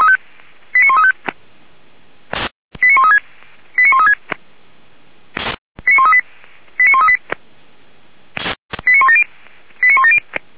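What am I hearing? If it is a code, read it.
Five-tone selective call (ZVEI), heard as quick runs of five tones, tones of about 70 ms. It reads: 89135 89136 89136 89138 89138 89170 89170
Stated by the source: ZVEI five-tone selective call received on SDR.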